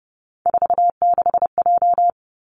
Morse code, heard as a keyed tone, about 30 wpm, 700 Hz. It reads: 46J
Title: Three Challenging Letter-Number Combinations 30wpm